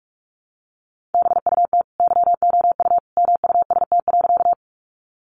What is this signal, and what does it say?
Morse code, heard as a keyed tone, about 44 wpm, 700 Hz. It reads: BUT YOU MUST.
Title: Sentences from Top 200 Words 45wpm